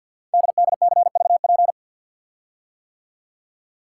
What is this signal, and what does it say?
Morse code, heard as a keyed tone, about 48 wpm, 690 Hz. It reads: DDCVP
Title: Code Groups - 5 Letters - 50wpm